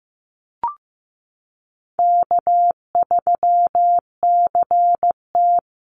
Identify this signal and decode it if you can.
Morse code, heard as a keyed tone, about 15 wpm, 700 Hz. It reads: K3CT
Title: Call Signs Repeated 2x - 15wpm